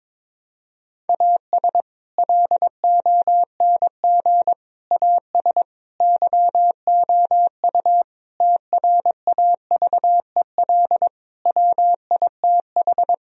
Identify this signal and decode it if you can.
Morse code, heard as a keyed tone, about 22 wpm, 700 Hz. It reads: AS LONG AS YOU TRAVEL WITH
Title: Sentences from Top 1000 Words - Review Part 2 - 22wpm